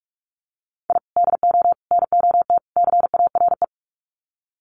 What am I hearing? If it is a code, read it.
Morse code, heard as a keyed tone, about 44 wpm, 700 Hz. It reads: I DO NOT CARE
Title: Sets of 4 Words 45wpm